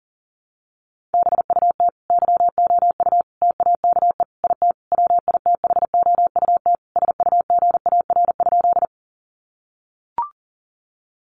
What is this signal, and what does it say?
Morse code, heard as a keyed tone, about 40 wpm, 700 Hz. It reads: BUT YOU TAKE IT WITHOUT SUGAR?
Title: Sentences from Top 1200 Words 40wpm